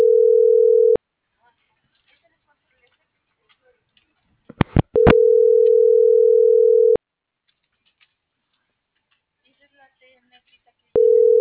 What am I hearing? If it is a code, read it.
Ringback tone (North America), 440 + 480 Hz.